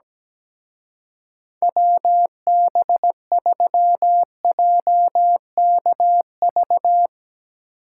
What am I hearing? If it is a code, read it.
Morse code, heard as a keyed tone, about 18 wpm, 700 Hz. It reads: WB3JKV
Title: Speed Racing - USA 2x3 Call Signs 17wpm